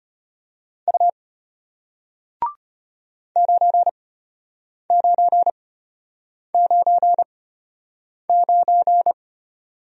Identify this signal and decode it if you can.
Morse code, heard as a keyed tone, about 34 wpm, spaced as for 8 wpm, 700 Hz. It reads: U 9999